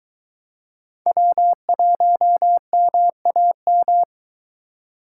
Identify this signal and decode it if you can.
Morse code, heard as a keyed tone, about 23 wpm, 700 Hz. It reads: W1MAM